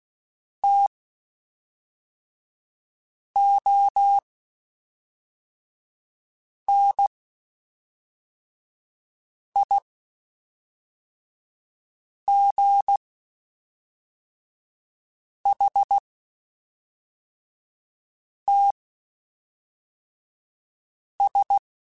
Morse code, heard as a keyed tone, about 16 wpm, spaced as for 3 wpm, 800 Hz. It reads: TONIGHTS
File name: Spaced Characters / Spaced Characters-mp3s / Weather 4 21 sp @5